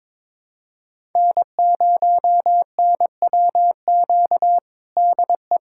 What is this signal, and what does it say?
Morse code, heard as a keyed tone, about 22 wpm, 700 Hz. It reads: N0NWQ DE
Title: USA Callsign DE Callsign - 5 Characters - 22wpm